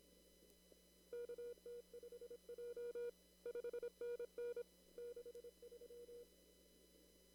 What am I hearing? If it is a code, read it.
Morse code, heard as a keyed tone, about 26 wpm, 480 Hz. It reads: KT5J 5NN 63
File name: CW / kt5j wpx cw-op-k5pi